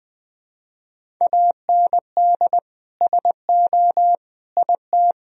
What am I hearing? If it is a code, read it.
Morse code, heard as a keyed tone, about 20 wpm, 700 Hz. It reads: AND SO IT
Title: Sentences from Top 1000 Words - Review Part 1 - 20wpm